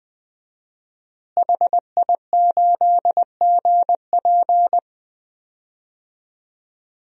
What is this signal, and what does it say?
Morse code, heard as a keyed tone, about 20 wpm, 700 Hz. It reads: HI8GP
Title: DXCC-100 Call Signs - 5 Characters - Repeated 2x - 20wpm